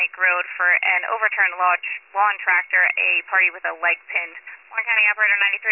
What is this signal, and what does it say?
Emergency Alert System (SAME) tones (header bursts), heard as harsh buzzing bursts, with speech. It is heard from 0.5 to 4.9 s.